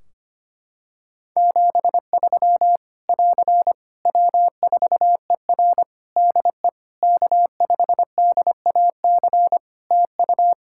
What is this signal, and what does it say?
Morse code, heard as a keyed tone, about 25 wpm, 700 Hz. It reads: 73 <AR> W4ER DE K5DAC TU
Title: K1USN SST QSO Practice 25wpm